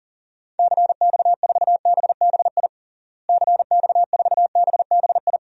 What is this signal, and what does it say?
Morse code, heard as a keyed tone, about 37 wpm, 700 Hz. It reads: CX4BBI CX4BBI
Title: DXCC-100 Call Signs - 6 Characters - Repeated 2x - 40wpm